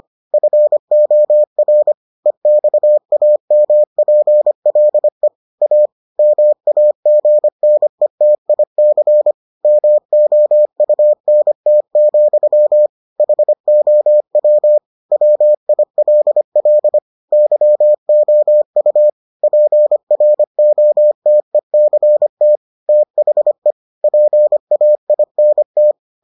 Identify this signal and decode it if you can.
Morse code, heard as a keyed tone, about 25 wpm, 600 Hz. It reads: FOR EXAMPLE A MAGNETIC MOUNT, HOW WILL YOU PROTECT THE PAINT